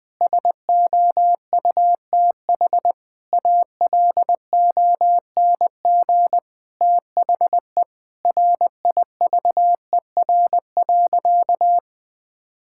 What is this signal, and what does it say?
Morse code, heard as a keyed tone, about 20 wpm, 700 Hz. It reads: SOUTH ALONG THE RIVER.
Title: Sentences from Top 1000 Words - Review Part 6 - 20wpm